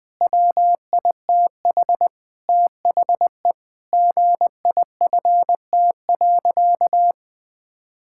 Morse code, heard as a keyed tone, about 19 wpm, 700 Hz. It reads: WITH THE GIFT.
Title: Sentences with 5-Word Phrases - 20wpm